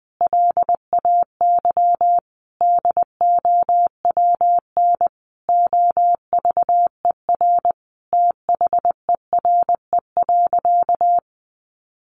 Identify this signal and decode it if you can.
Morse code, heard as a keyed tone, about 19 wpm, 700 Hz. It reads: LAY DOWN OVER THERE.
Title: Sentences from Top 800 Words 20wpm